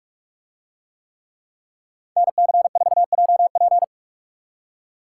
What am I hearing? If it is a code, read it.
Morse code, heard as a keyed tone, about 44 wpm, 700 Hz. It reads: NC4JP